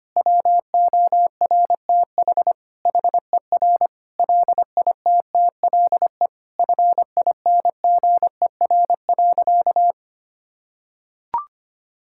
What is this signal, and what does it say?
Morse code, heard as a keyed tone, about 25 wpm, 700 Hz. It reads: WORTH HER LITTLE FINGER.
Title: Sentences from Top 1600 Words 25wpm